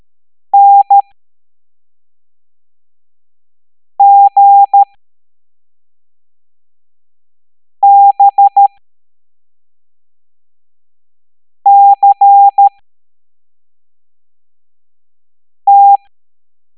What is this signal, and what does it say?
Morse code, heard as a keyed tone, about 13 wpm, spaced as for 3 wpm, 800 Hz. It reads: NGBCT